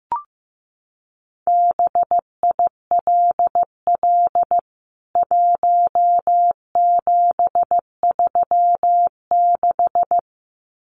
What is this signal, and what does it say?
Morse code, heard as a keyed tone, about 15 wpm, 700 Hz. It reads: BILL 1736
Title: CWT Exchange 15wpm